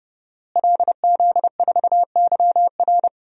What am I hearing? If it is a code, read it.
Morse code, heard as a keyed tone, about 30 wpm, 700 Hz. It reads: LZ4YR
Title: DXCC-100 Call Signs - 5 Characters - Repeated 3x - 30wpm